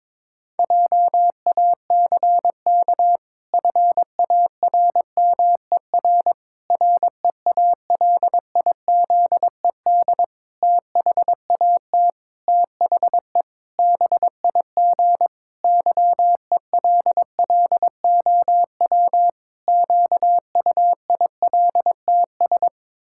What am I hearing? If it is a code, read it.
Morse code, heard as a keyed tone, about 22 wpm, 700 Hz. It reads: JACK FARMER REALIZED THAT THE BIG YELLOW QUILTS
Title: Pangram Challenge v2 - 22wpm